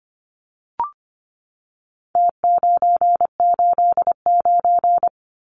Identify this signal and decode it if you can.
Morse code, heard as a keyed tone, about 25 wpm, 700 Hz. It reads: T989
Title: Numbers - Contest Style - 4-Digit 25wpm